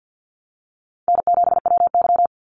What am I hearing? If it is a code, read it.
Morse code, heard as a keyed tone, about 49 wpm, 700 Hz. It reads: N7WY
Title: Easy Call Signs Repeated 3x - 50wpm